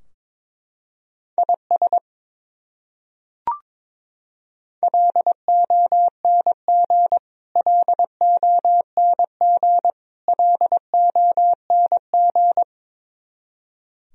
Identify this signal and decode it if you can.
Morse code, heard as a keyed tone, about 22 wpm, 700 Hz. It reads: IS LONG LONG LONG
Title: Top 100 Words Repeated 22wpm